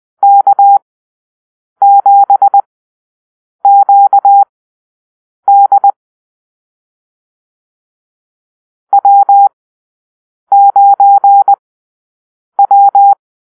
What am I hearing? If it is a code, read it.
Morse code, heard as a keyed tone, about 20 wpm, spaced as for 7 wpm, 800 Hz. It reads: K7QD W9W
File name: arrl / Call-Sign Drill-8